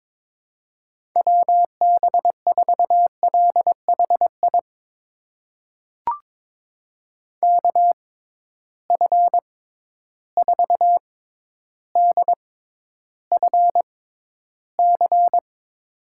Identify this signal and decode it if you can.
Morse code, heard as a keyed tone, about 22 wpm, spaced as for 11 wpm, 700 Hz. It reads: WB4LHI KF4DFC
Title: USA Call Signs - 6 Characters - 6x Character Spacing - 22wpm